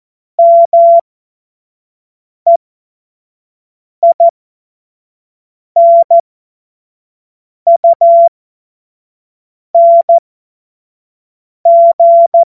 Morse code, heard as a keyed tone, about 14 wpm, spaced as for 5 wpm, 680 Hz. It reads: MEINUNG
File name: Morsekurs / Witze / Witz 31. Januar 2023 25 F70